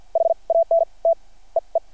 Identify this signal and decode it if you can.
Morse code, heard as a keyed tone, about 51 wpm, spaced as for 34 wpm, 660 Hz. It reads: HANT EE
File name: archived / CRSnet